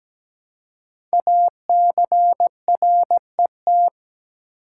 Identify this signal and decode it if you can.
Morse code, heard as a keyed tone, about 17 wpm, 700 Hz. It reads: ACRET
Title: Code Groups - 5 Letters - 17wpm